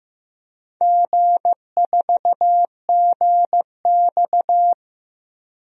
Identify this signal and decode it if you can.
Morse code, heard as a keyed tone, about 15 wpm, 700 Hz. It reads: G4GX
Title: DXCC-100 Call Signs - 4 Characters - 15wpm